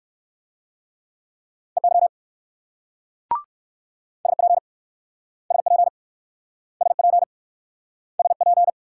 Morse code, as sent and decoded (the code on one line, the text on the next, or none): . -.-- / ... .--. / ... .--. / ... .--. / ... .--.
EY SP SP SP SP